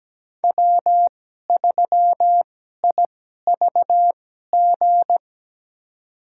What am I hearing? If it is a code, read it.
Morse code, heard as a keyed tone, about 17 wpm, spaced as for 12 wpm, 700 Hz. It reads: W3IVG
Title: USA Call Signs - 5 Characters - 2x Character Spacing - 17wpm